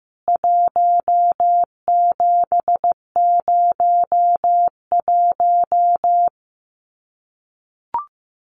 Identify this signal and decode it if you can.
Morse code, heard as a keyed tone, about 15 wpm, 700 Hz. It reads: 1701
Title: CWT Exchange 15wpm